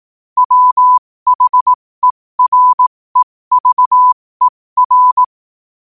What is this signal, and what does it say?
Morse code, heard as a keyed tone, about 18 wpm, spaced as for 15 wpm, 1000 Hz. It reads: WHEREVER
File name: qotdfiles / 15wpm